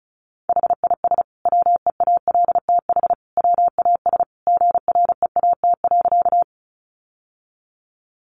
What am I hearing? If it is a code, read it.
Morse code, heard as a keyed tone, about 34 wpm, 700 Hz. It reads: HIS WEALTH WAS GREAT.
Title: Sentences from Top 1200 Words 35wpm